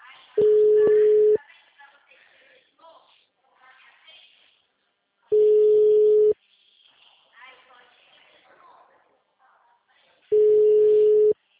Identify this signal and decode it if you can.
Ringback tone (Europe, 425 Hz), with speech.